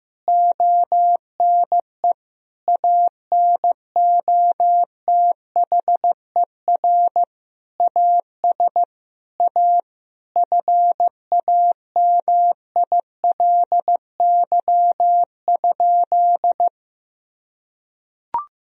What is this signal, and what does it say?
Morse code, heard as a keyed tone, about 15 wpm, 700 Hz. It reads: ONE ANOTHER AS A FAMILY?
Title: Sentences from Top 1000 Words - Review Part 1 - 15wpm